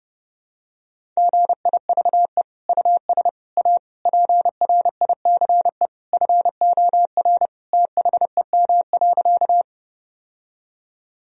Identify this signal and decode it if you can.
Morse code, heard as a keyed tone, about 32 wpm, 700 Hz. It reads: GIVE US A PRICE FOR THEM.
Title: Sentences from Top 1000 Words - Review Part 4 - 30wpm